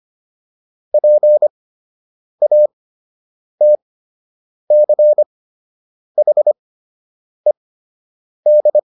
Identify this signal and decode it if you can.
Morse code, heard as a keyed tone, about 25 wpm, spaced as for 8 wpm, 600 Hz. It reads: PATCHED